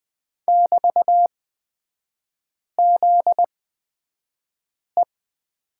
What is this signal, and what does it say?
Morse code, heard as a keyed tone, about 20 wpm, spaced as for 5 wpm, 700 Hz. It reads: <BT>ZE
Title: 243 Lesson 39e - Letter z - Three Characters 20wpm